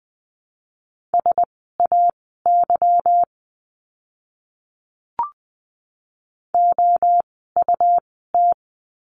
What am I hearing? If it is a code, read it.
Morse code, heard as a keyed tone, about 20 wpm, spaced as for 14 wpm, 700 Hz. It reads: SAY OUT
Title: 230 Lesson 37f - Letter x - Words 20wpm